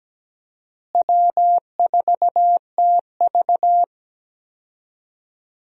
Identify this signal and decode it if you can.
Morse code, heard as a keyed tone, about 17 wpm, 700 Hz. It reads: W4TV
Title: USA Call Signs - 4 Characters - Repeated 2x - 17wpm